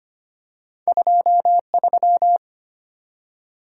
Morse code, heard as a keyed tone, about 25 wpm, 700 Hz. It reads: 23
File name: Two Letter-Number Combinations - 25wpm